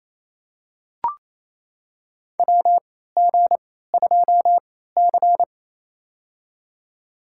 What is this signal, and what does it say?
Morse code, heard as a keyed tone, about 28 wpm, spaced as for 16 wpm, 700 Hz. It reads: WG2C
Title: USA Call Signs - 3x Character Spacing - 28wpm